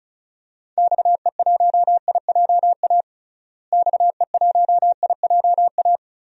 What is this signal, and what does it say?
Morse code, heard as a keyed tone, about 36 wpm, 700 Hz. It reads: XE1IJA XE1IJA
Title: DXCC-100 Call Signs - 6 Characters - Repeated 2x - 35wpm